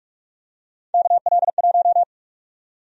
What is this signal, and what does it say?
Morse code, heard as a keyed tone, about 42 wpm, 700 Hz. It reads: KL1
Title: USA Call Sign Prefixes - 45wpm